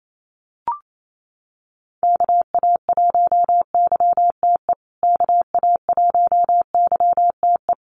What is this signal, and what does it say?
Morse code, heard as a keyed tone, about 28 wpm, 700 Hz. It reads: KA1YTE KA1YTE
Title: USA Call Signs Repeated 3x 28wpm